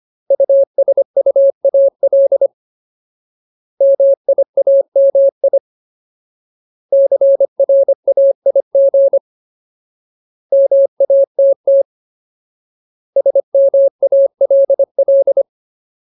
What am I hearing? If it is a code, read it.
Morse code, heard as a keyed tone, about 25 wpm, 550 Hz. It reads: USUAL MIAMI CRAIG MATT SMALL
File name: Ultimate Word List / 25WPM Ultimate